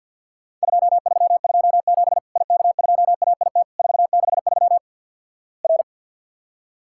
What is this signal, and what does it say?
Morse code, heard as a keyed tone, about 53 wpm, 700 Hz. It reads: 2327 IK2AIT 463 R
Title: Worked All Europe DX Contest - QTC Exchange - 50wpm